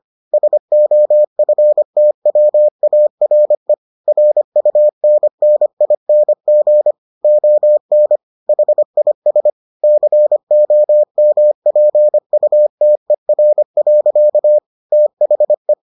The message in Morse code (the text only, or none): SOFTWARE RUNNING ON HIS COMPUTER. THE